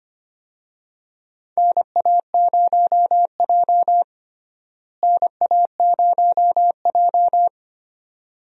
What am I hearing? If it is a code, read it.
Morse code, heard as a keyed tone, about 25 wpm, 700 Hz. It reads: NA0J NA0J
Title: USA Call Signs - 4 Characters - Repeated 2x - 25wpm